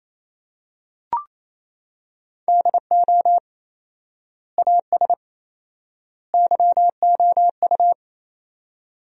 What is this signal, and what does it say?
Morse code, heard as a keyed tone, about 28 wpm, 700 Hz. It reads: DO AS YOU
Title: 4x Word-Spacing - Sentences from Top 100 Words - 28wpm